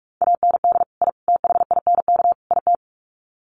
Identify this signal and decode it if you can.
Morse code, heard as a keyed tone, about 44 wpm, 700 Hz. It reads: AND I THINK IT
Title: Sets of 4 Words 45wpm